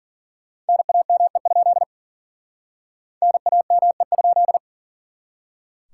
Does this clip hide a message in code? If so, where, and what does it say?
Yes: Morse code from 0.7 to 4.6 s, heard as a keyed tone, about 52 wpm, 700 Hz. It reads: NAME? NAME?